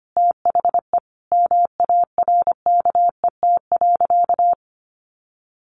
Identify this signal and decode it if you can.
Morse code, heard as a keyed tone, about 25 wpm, 700 Hz. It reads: THE MARKET.
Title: Sentences from Top 1400 Words 25wpm